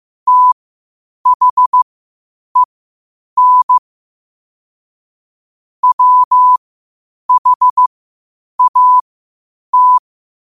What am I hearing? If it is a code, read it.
Morse code, heard as a keyed tone, about 15 wpm, spaced as for 8 wpm, 1000 Hz. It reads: THEN WHAT
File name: qotdfiles / 8wpm